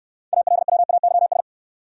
Morse code, heard as a keyed tone, about 63 wpm, 700 Hz. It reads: ALWAYS